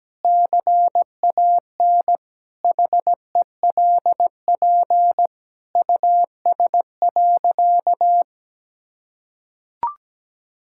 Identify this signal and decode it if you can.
Morse code, heard as a keyed tone, about 17 wpm, 700 Hz. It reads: CAN HELP US.
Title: Sentences from Top 1000 Words - Review Part 5 - 17wpm